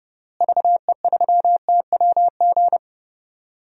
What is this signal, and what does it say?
Morse code, heard as a keyed tone, about 30 wpm, 700 Hz. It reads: VE3TWG
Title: DXCC-100 Call Signs - 6 Characters - Repeated 3x - 30wpm